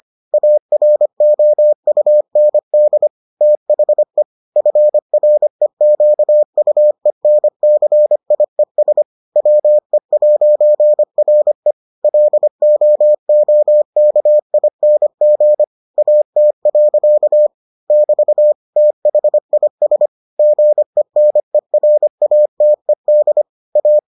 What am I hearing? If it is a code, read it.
Morse code, heard as a keyed tone, about 25 wpm, 600 Hz. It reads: AROUND THE FREQUENCIES WE'RE LOOKING AT. <BT> THIS GENERATED A